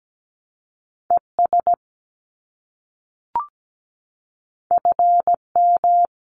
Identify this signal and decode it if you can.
Morse code, heard as a keyed tone, about 17 wpm, 700 Hz. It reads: ES FM
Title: Abbreviations Repeated 3x - 17wpm